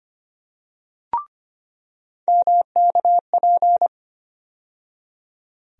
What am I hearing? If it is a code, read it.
Morse code, heard as a keyed tone, about 25 wpm, 700 Hz. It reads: MKP
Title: USA Call Sign Postfixes - 3 Letters 25wpm